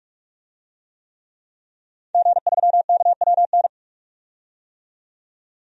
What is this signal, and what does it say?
Morse code, heard as a keyed tone, about 46 wpm, 700 Hz. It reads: G3KWN
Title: DXCC-100 Call Signs - 5 Characters - 45wpm